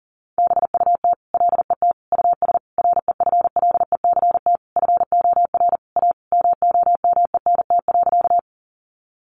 Morse code, heard as a keyed tone, about 40 wpm, 700 Hz. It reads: BUT LET US REFLECT FOR A MOMENT.
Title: Sentences from Top 1500 Words 40wpm